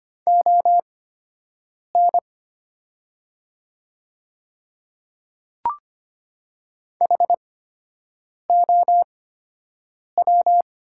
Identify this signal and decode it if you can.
Morse code, heard as a keyed tone, about 25 wpm, spaced as for 7 wpm, 700 Hz. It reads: ON HOW
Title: 8x Character Spacing - Top 200 Words - 25wpm